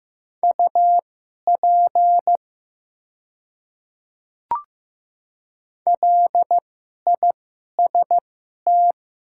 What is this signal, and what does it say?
Morse code, heard as a keyed tone, about 15 wpm, spaced as for 11 wpm, 700 Hz. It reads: UP LIST
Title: 2x Character Spacing - Top 500 Words - 15wpm